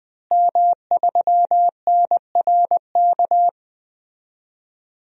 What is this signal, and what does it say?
Morse code, heard as a keyed tone, about 20 wpm, 700 Hz. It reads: M3NRK
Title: DXCC-100 Call Signs - 5 Characters - Repeated 3x - 20wpm